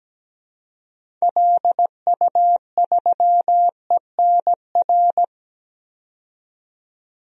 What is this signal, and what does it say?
Morse code, heard as a keyed tone, about 17 wpm, 700 Hz. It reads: LU3ENR
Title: DXCC-100 Call Signs - 6 Characters - 17wpm